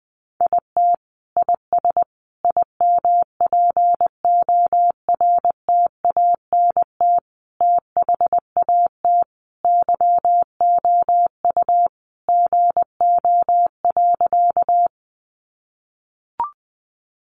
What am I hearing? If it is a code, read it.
Morse code, heard as a keyed tone, about 20 wpm, 700 Hz. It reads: IT IS IMPORTANT THAT YOU GO.
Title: Sentences from Top 600 Words 20wpm